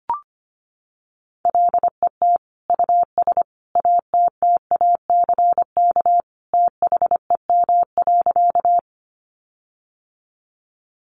Morse code, heard as a keyed tone, about 25 wpm, 700 Hz. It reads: LET US ATTACK THEM.
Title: Sentences from Top 1500 Words 25wpm